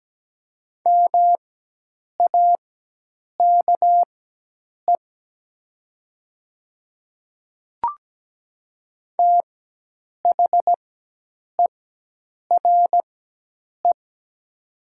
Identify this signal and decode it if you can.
Morse code, heard as a keyed tone, about 17 wpm, spaced as for 8 wpm, 700 Hz. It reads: MAKE THERE
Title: 4x Character Spacing - Top 500 Words - 17wpm